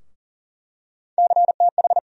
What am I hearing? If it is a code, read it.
Morse code, heard as a keyed tone, about 40 wpm, 700 Hz. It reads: CTH
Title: USA Call Sign Postfixes - 3 Letters 40wpm